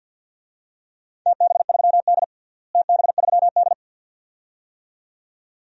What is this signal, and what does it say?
Morse code, heard as a keyed tone, about 52 wpm, 700 Hz. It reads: TB3D TB3D
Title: DXCC-100 Call Signs - 4 Characters - Repeated 2x - 50wpm